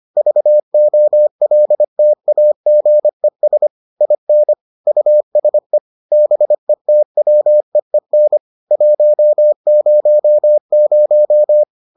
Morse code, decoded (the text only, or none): VOLTAGES IN USE BETWEEN 100